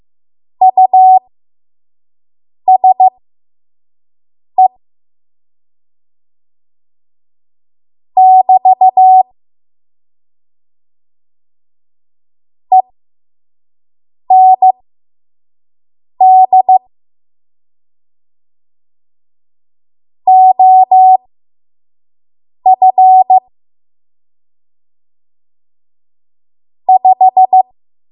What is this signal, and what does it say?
Morse code, heard as a keyed tone, about 15 wpm, spaced as for 5 wpm, 750 Hz. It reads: USE <BT> END OF 5
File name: Morse / 05WPM